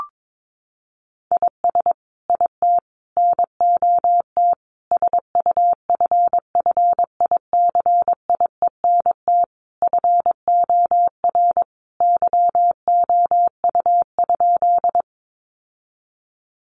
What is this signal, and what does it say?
Morse code, heard as a keyed tone, about 22 wpm, 700 Hz. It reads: IS IT NOT SUFFICIENT FOR YOU?
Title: Sentences from Top 1400 Words 22wpm